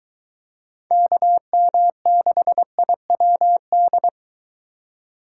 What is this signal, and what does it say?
Morse code, heard as a keyed tone, about 23 wpm, 700 Hz. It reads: KM6IWD